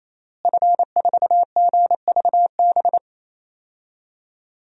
Morse code, heard as a keyed tone, about 27 wpm, 700 Hz. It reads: F4GVB
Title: DXCC-100 Call Signs - 5 Characters - Repeated 3x - 28wpm